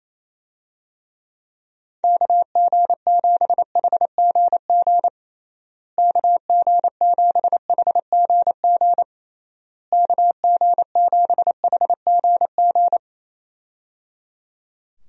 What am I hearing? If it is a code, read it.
Morse code, heard as a keyed tone, about 27 wpm, 700 Hz. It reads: KG7HGG KG7HGG KG7HGG
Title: USA Call Signs - 6 Characters - Repeated 3x - 28wpm